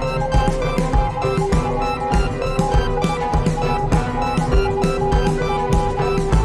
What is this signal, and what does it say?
Morse code, heard as a keyed tone, about 20 wpm, 820 Hz. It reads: A NEW TURNING